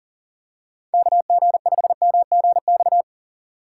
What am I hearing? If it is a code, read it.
Morse code, heard as a keyed tone, about 40 wpm, 700 Hz. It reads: KG5MGX
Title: USA Call Signs - 6 Characters - 3x Character Spacing - 40wpm